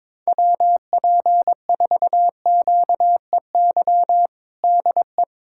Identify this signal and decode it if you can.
Morse code, heard as a keyed tone, about 22 wpm, 700 Hz. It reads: WP4QEY DE
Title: USA Callsign DE Callsign 22wpm